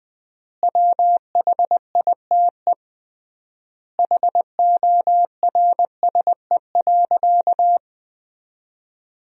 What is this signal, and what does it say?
Morse code, heard as a keyed tone, about 20 wpm, 700 Hz. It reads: WHITE HORSE.
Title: 3x Word-Spacing - Sentences from Top 300 Words - 20wpm